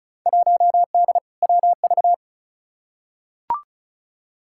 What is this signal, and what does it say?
Morse code, heard as a keyed tone, about 34 wpm, 700 Hz. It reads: 1D WV
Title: Field Day Exchanges - 35wpm